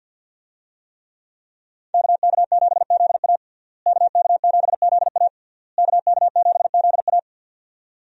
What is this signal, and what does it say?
Morse code, heard as a keyed tone, about 52 wpm, 700 Hz. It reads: KK7ZA KK7ZA KK7ZA
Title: USA Call Signs - 5 Characters - Repeated 3x - 50wpm